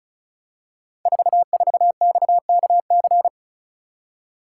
Morse code, heard as a keyed tone, about 36 wpm, 700 Hz. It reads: 44XKC
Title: Code Groups - 5 Letter:Numbers - 35wpm